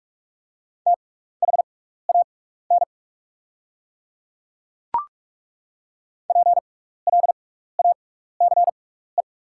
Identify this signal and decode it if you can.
Morse code, heard as a keyed tone, about 42 wpm, spaced as for 15 wpm, 690 Hz. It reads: THAN PLACE
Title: 6x Character Spacing - Top 200 Words - 45wpm